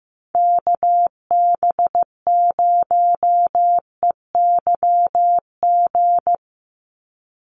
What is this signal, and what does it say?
Morse code, heard as a keyed tone, about 15 wpm, 700 Hz. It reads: KB0EYG
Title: USA Call Signs 15wpm